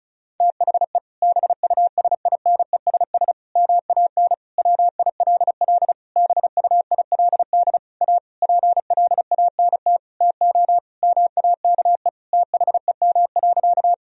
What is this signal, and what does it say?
Morse code, heard as a keyed tone, about 34 wpm, 700 Hz. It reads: THE BUSINESS MAN WILL BUILD A PLANT TO MAKE THEM.
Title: Sentences from Top 1000 Words - Review Part 1 - 35wpm